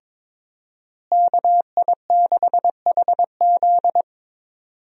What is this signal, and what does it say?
Morse code, heard as a keyed tone, about 22 wpm, 700 Hz. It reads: KI6HZ
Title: USA Call Signs - 5 Characters - Repeated 3x - 22wpm